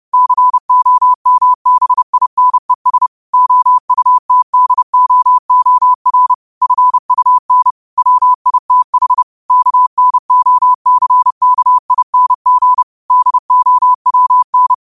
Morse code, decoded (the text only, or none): COMBINES OUTDOOR FUN WITH KNOCKING DOWN